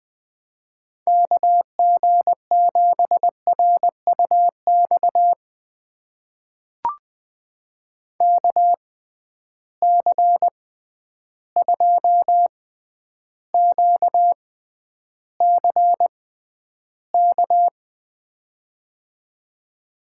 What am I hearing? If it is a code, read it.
Morse code, heard as a keyed tone, about 20 wpm, spaced as for 10 wpm, 700 Hz. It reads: KG7RUX KC2QCK
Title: USA Call Signs - 6 Characters - 6x Character Spacing - 20wpm